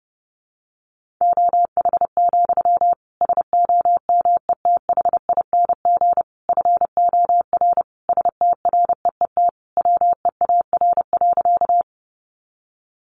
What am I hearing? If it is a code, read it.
Morse code, heard as a keyed tone, about 30 wpm, 700 Hz. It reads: OH, SOMETHING FOR STREET WEAR.
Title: Sentences from Top 1500 Words 30wpm